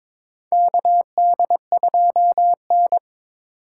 Morse code, heard as a keyed tone, about 22 wpm, 700 Hz. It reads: KD2N
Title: USA Call Signs - 4 Characters - 2x Character Spacing - 22wpm